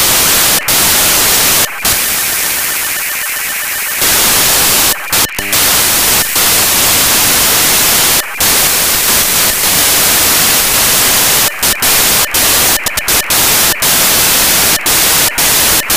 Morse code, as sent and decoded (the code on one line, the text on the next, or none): none